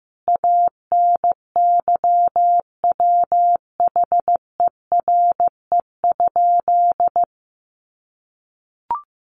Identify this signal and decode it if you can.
Morse code, heard as a keyed tone, about 14 wpm, 700 Hz. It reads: ANYWHERE?
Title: Sentences from Top 1400 Words 15wpm